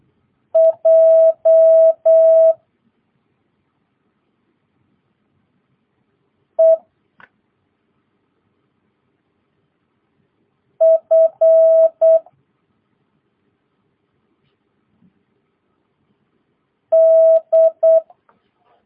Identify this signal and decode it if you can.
Morse code, heard as a keyed tone, about 8 wpm, 650 Hz. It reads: J E F D
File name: jh8cbh / 2022cw